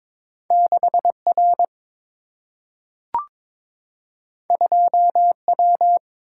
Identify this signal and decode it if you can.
Morse code, heard as a keyed tone, about 22 wpm, 700 Hz. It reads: 6 R 2 W
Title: Single-Letter Single-Number 22wpm